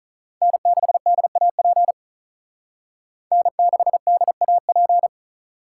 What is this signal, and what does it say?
Morse code, heard as a keyed tone, about 44 wpm, 700 Hz. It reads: N6DAP N6DAP